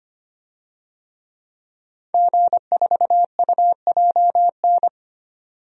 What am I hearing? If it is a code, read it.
Morse code, heard as a keyed tone, about 25 wpm, 700 Hz. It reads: G4UJN
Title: DXCC-100 Call Signs - 5 Characters - 25wpm